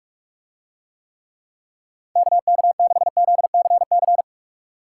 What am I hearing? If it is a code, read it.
Morse code, heard as a keyed tone, about 41 wpm, 700 Hz. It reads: KK6ZCC